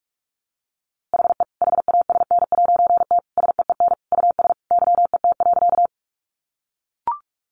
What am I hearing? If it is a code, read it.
Morse code, heard as a keyed tone, about 44 wpm, 700 Hz. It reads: HE HASN'T SEEN US YET.
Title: A Sound Of Thunder 45wpm